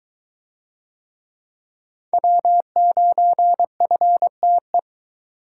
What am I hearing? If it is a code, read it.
Morse code, heard as a keyed tone, about 22 wpm, 700 Hz. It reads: W9FTE